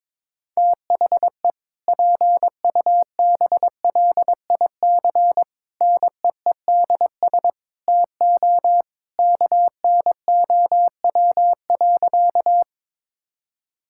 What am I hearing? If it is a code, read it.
Morse code, heard as a keyed tone, about 21 wpm, 700 Hz. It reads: THE PUBLIC NEEDS TO KNOW.
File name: practice / 22wpm / Sentences with 2-Word Phrases - 22wpm